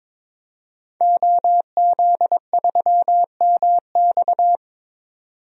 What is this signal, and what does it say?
Morse code, heard as a keyed tone, about 22 wpm, 700 Hz. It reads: OZ3MX
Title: DXCC-100 Call Signs - 5 Characters - 22wpm